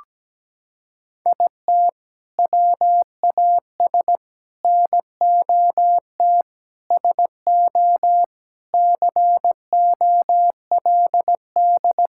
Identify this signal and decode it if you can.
Morse code, heard as a keyed tone, about 17 wpm, 700 Hz. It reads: IT WAS NOT SO COLD
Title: Sentences from Top 1600 Words 17wpm